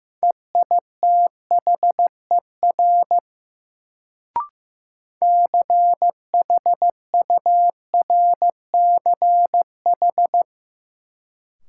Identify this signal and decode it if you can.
Morse code, heard as a keyed tone, about 15 wpm, 700 Hz. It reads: EITHER CHURCH
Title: Sets of 5 Words - New Word Drill - 15wpm